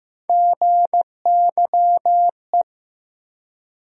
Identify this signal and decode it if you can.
Morse code, heard as a keyed tone, about 16 wpm, 700 Hz. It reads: GYE
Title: USA Call Sign Postfixes - 3 Letters 15wpm